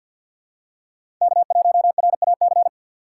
Morse code, heard as a keyed tone, about 48 wpm, 700 Hz. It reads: K1RAC
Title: USA Call Signs - 5 Characters - 2x Character Spacing - 50wpm